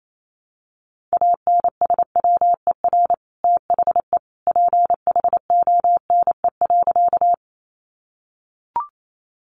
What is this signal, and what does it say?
Morse code, heard as a keyed tone, about 28 wpm, 700 Hz. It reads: ANSWER THE PHONE.
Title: Sentences from Top 1100 Words 28wpm